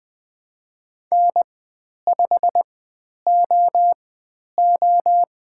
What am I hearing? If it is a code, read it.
Morse code, heard as a keyed tone, about 20 wpm, spaced as for 10 wpm, 700 Hz. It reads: N5OO
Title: 095 Lesson 16g - Letter c - Call Signs 20wpm